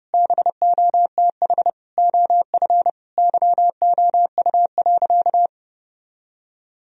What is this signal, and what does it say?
Morse code, heard as a keyed tone, about 32 wpm, 700 Hz. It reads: BOTH OF YOU.
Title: Sentences from Top 1000 Words - Review Part 1 - 30wpm